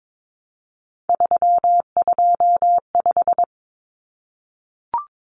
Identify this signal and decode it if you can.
Morse code, heard as a keyed tone, about 22 wpm, 700 Hz. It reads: 325
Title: Numbers - 3-Digit 22wpm